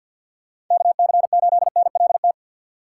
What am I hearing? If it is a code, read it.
Morse code, heard as a keyed tone, about 53 wpm, 700 Hz. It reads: KC8NLT
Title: USA Call Signs - 6x Character Spacing - 50wpm